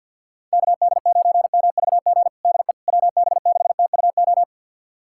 Morse code, heard as a keyed tone, about 49 wpm, 700 Hz. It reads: KD9MVG DE WB6TUO
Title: USA Callsign DE Callsign 50wpm